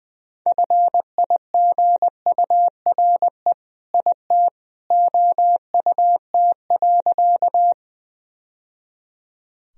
Morse code, heard as a keyed tone, about 20 wpm, 700 Hz. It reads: FIGURE IT OUT.
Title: Sentences from Top 1000 Words - Review Part 2 - 20wpm